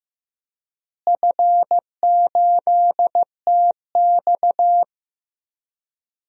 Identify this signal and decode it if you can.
Morse code, heard as a keyed tone, about 15 wpm, 700 Hz. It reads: F8TX
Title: DXCC-100 Call Signs - 4 Characters - Repeated 2x - 15wpm